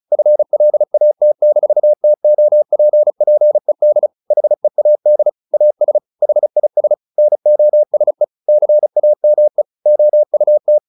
Morse code, heard as a keyed tone, about 34 wpm, 600 Hz. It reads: FLAT-TOPPED HEAD AS HIS NOSE CAME OUT
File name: the old man and the sea 2 35wpm